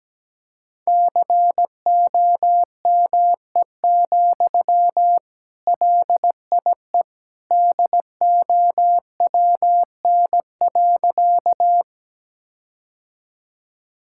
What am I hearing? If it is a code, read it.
Morse code, heard as a keyed tone, about 17 wpm, 700 Hz. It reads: COME, LIE DOWN.
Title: Sentences from Top 1600 Words 17wpm